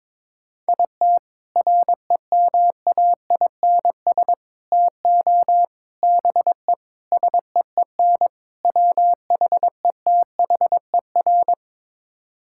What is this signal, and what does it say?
Morse code, heard as a keyed tone, about 22 wpm, 700 Hz. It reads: IT REMAINS TO BE SEEN WHETHER
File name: practice / 22wpm / Sets of 6 Words - 22wpm